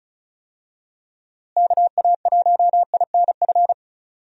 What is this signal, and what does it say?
Morse code, heard as a keyed tone, about 34 wpm, 700 Hz. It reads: KA1INF
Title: USA Call Signs - 6 Characters - Repeated 3x - 35wpm